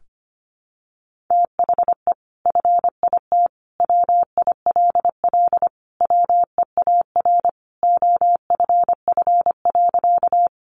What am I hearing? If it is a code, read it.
Morse code, heard as a keyed tone, about 25 wpm, 700 Hz. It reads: THE FIT WILL WEAR OFF.
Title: Sentences from Top 1500 Words 25wpm